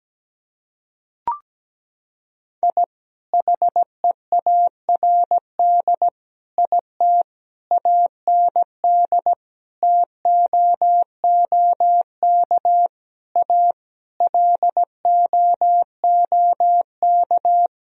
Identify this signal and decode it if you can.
Morse code, heard as a keyed tone, about 17 wpm, 700 Hz. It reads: I HEARD IT AND TOOK A LOOK
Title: Sentences from Top 1000 Words - Review Part 1 - 17wpm